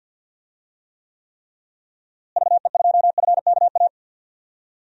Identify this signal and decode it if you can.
Morse code, heard as a keyed tone, about 52 wpm, 700 Hz. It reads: VE2FKA
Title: DXCC-100 Call Signs - 6 Characters - Repeated 3x - 50wpm